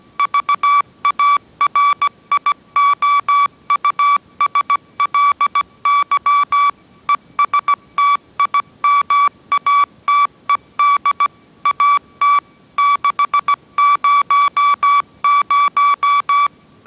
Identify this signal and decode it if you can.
Morse code, heard as a keyed tone, about 17 wpm, 1150 Hz. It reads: VARIOUSLY ESTIMATED AT 600
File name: 18wpm